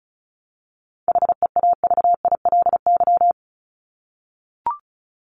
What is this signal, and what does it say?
Morse code, heard as a keyed tone, about 34 wpm, 700 Hz. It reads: HEAVILY
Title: LY Suffix - Top 100 Words with Base Words 35wpm